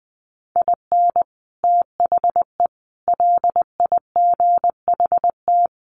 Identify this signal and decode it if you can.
Morse code, heard as a keyed tone, about 20 wpm, 700 Hz. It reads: IN THE LIGHT
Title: Sets of 5 Words Encore2 20wpm